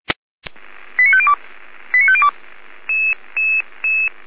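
Five-tone selective call (ZVEI), heard as quick runs of five tones, tones of about 80 ms. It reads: 89502 89502
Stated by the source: ZVEI five-tone selective call 89329, received on SDR, from the Feuerwehr Orlamünde audio server.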